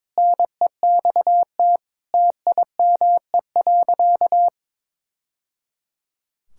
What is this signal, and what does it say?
Morse code, heard as a keyed tone, about 22 wpm, 700 Hz. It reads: NEXT TIME.
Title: Sentences from Top 1000 Words - Review Part 6 - 22wpm